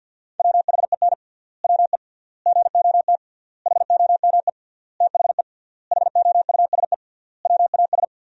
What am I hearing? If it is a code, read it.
Morse code, heard as a keyed tone, about 49 wpm, 700 Hz. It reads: WHEN WE GOT HOME THE HOUSE WAS
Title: 3x Word-Spacing - Sentences from Top 500 Words - 50wpm